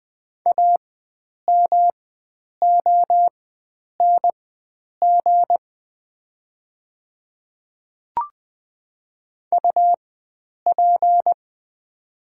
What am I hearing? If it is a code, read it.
Morse code, heard as a keyed tone, about 20 wpm, spaced as for 9 wpm, 700 Hz. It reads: AMONG UP
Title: 4x Character Spacing - Top 500 Words - 20wpm